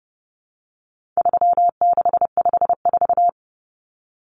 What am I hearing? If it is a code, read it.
Morse code, heard as a keyed tone, about 30 wpm, 700 Hz. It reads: 3654
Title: Common Cut Numbers - 4-Digit 30wpm